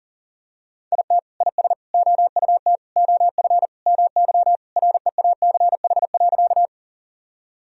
Morse code, heard as a keyed tone, about 40 wpm, 700 Hz. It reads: IT IS OUT OF MY REACH.
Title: Sentences from Top 1000 Words - Review Part 2 - 40wpm